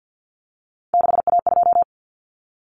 Morse code, heard as a keyed tone, about 50 wpm, 700 Hz. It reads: 6A2
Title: Common Cut Numbers - 3-Digit 50wpm